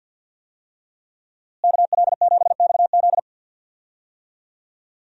Morse code, heard as a keyed tone, about 47 wpm, 700 Hz. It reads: KL7XZ
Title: DXCC-100 Call Signs - 5 Characters - Repeated 3x - 50wpm